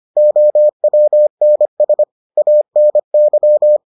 Morse code, heard as a keyed tone, about 25 wpm, 600 Hz. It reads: OWNS ANY